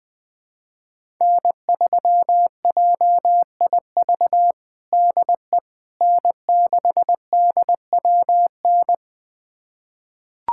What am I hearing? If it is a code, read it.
Morse code, heard as a keyed tone, about 21 wpm, 700 Hz. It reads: N3JIV DE N6DWN
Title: USA Callsign DE Callsign - 5 Characters - 20wpm